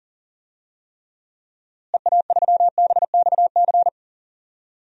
Morse code, heard as a keyed tone, about 40 wpm, 700 Hz. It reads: EA3BXC
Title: DXCC-100 Call Signs - 6 Characters - Repeated 3x - 40wpm